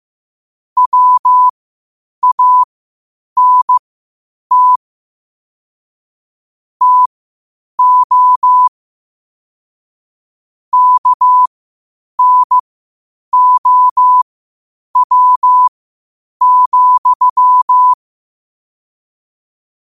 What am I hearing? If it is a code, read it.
Morse code, heard as a keyed tone, about 15 wpm, spaced as for 8 wpm, 1000 Hz. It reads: WANT TO KNOW,